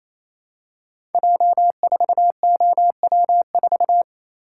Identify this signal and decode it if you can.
Morse code, heard as a keyed tone, about 27 wpm, 700 Hz. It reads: J4OW4
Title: Code Groups - 5 Letter:Numbers - 28wpm